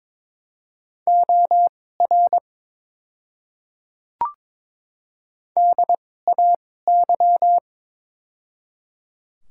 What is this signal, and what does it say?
Morse code, heard as a keyed tone, about 22 wpm, spaced as for 16 wpm, 700 Hz. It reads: OR DAY
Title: 2x Character Spacing - Top 100 Words - 22wpm